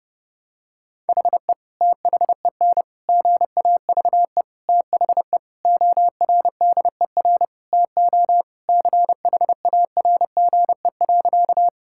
Morse code, heard as a keyed tone, about 29 wpm, 700 Hz. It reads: HE THEN GAVE THE ORDER TO CHARGE.
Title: Sentences from Top 1000 Words - Review Part 6 - 30wpm